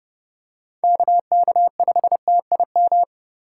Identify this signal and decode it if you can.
Morse code, heard as a keyed tone, about 28 wpm, 700 Hz. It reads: KK5TIM